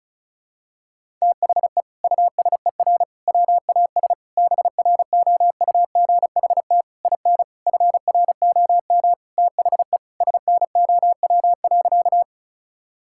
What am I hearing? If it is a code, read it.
Morse code, heard as a keyed tone, about 36 wpm, 700 Hz. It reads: THE USER WAS BROUGHT IN FROM THE SNOW.
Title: Sentences from Top 1000 Words - Review Part 5 - 35wpm